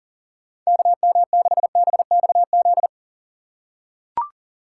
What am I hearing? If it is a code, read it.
Morse code, heard as a keyed tone, about 40 wpm, 700 Hz. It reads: KM6BXZ
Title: USA Call Signs - 6 Characters - 3x Character Spacing - 40wpm